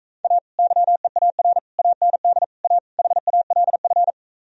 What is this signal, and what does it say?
Morse code, heard as a keyed tone, about 45 wpm, 700 Hz. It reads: A YEAR AND A HALF